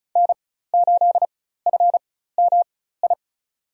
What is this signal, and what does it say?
Morse code, heard as a keyed tone, about 34 wpm, spaced as for 16 wpm, 700 Hz. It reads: N8FMI
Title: USA Call Signs - 5 Characters - 4x Character Spacing - 35wpm